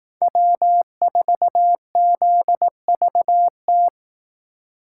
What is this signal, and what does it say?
Morse code, heard as a keyed tone, about 18 wpm, 700 Hz. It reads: W4ZVT
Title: Speed Racing - USA 1x3 Call Signs 15wpm